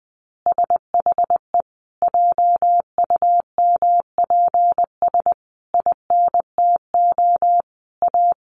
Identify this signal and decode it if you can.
Morse code, heard as a keyed tone, about 20 wpm, 700 Hz. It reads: SHE JUMPS INTO A
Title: Additional Easy Phrases 20wpm